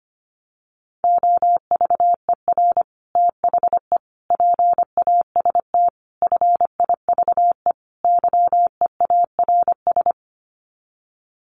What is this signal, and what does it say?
Morse code, heard as a keyed tone, about 25 wpm, 690 Hz. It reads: OVER THE PAST FIVE YEARS
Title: Sets of 5 Words Encore 25wpm